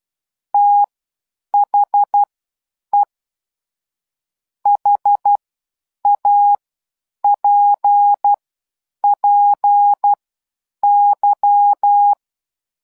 Morse code, heard as a keyed tone, about 12 wpm, spaced as for 8 wpm, 820 Hz. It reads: THE HAPPY